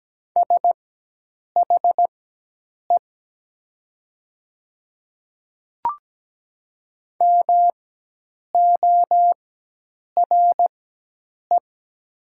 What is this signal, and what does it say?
Morse code, heard as a keyed tone, about 17 wpm, spaced as for 8 wpm, 700 Hz. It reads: SHE MORE
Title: 4x Character Spacing - Top 100 Words - 17wpm